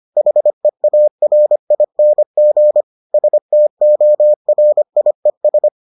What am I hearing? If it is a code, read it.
Morse code, heard as a keyed tone, about 25 wpm, 600 Hz. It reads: HEARING STORIES